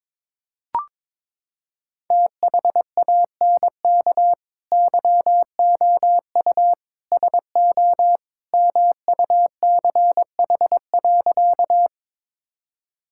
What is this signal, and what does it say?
Morse code, heard as a keyed tone, about 22 wpm, 700 Hz. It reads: THANK YOU SO MUCH.
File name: practice / 22wpm / Sentences with 2-Word Phrases - 22wpm